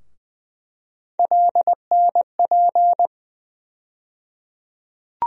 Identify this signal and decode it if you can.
Morse code, heard as a keyed tone, about 20 wpm, 700 Hz. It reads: LNP
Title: USA Call Sign Postfixes - 3 Letters 20wpm